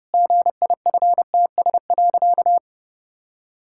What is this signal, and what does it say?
Morse code, heard as a keyed tone, about 30 wpm, 700 Hz. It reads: GIFTS.
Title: Pangram Challenge v2 - 30wpm